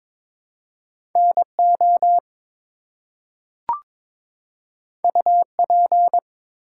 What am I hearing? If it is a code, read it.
Morse code, heard as a keyed tone, about 22 wpm, 700 Hz. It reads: NO UP
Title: Two Letter Words - 22wpm